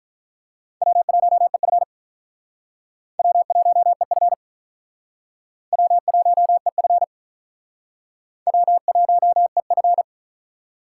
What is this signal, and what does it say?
Morse code, heard as a keyed tone, about 45 wpm, 700 Hz. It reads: W1EF W1EF W1EF W1EF